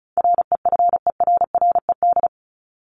Morse code, heard as a keyed tone, about 34 wpm, 700 Hz. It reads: REFERRED
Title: ED Suffix - Top 100 Words with Base Words 35wpm